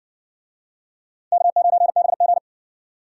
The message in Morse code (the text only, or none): K9BG